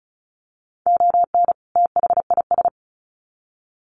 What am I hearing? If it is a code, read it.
Morse code, heard as a keyed tone, about 34 wpm, 700 Hz. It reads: ON THIS
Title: Sets of 2 Words 35wpm